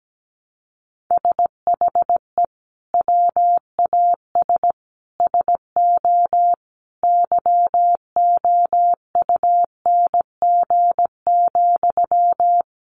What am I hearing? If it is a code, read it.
Morse code, heard as a keyed tone, about 17 wpm, 700 Hz. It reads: SHE WAS SO YOUNG,
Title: Sentences from Top 400 Words 17wpm